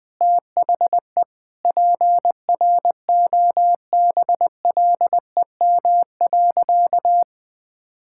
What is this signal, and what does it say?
Morse code, heard as a keyed tone, about 20 wpm, 700 Hz. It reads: THE PROBLEM.
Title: Sentences with 7-Word Phrases - 20wpm